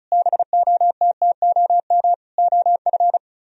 Morse code, heard as a keyed tone, about 43 wpm, 700 Hz. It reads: BOTTOM OF
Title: Speed-Racing - Sets of 3 Words 35wpm